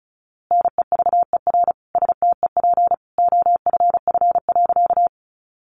Morse code, heard as a keyed tone, about 34 wpm, 700 Hz. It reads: NEVER STEP OFF.
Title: A Sound Of Thunder 35wpm